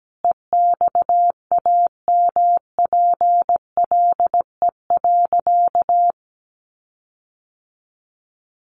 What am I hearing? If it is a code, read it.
Morse code, heard as a keyed tone, about 17 wpm, 700 Hz. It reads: EXAMPLE.
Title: Sentences from Top 1500 Words 17wpm